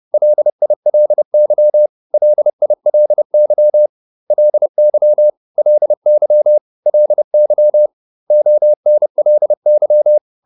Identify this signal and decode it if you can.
Morse code, heard as a keyed tone, about 30 wpm, 600 Hz. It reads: LILY LILY LY LY LY ONLY